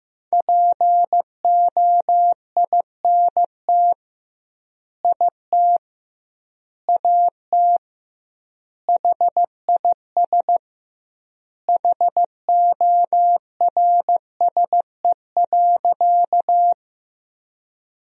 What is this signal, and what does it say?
Morse code, heard as a keyed tone, about 16 wpm, 700 Hz. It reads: POINT IT AT HIS HORSE.
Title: 2x Word-Spacing - Sentences from Top 300 Words - 15wpm